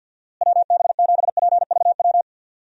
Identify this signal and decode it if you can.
Morse code, heard as a keyed tone, about 52 wpm, 700 Hz. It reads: WB7PVW
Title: USA Call Signs - 6 Characters - Repeated 2x - 50wpm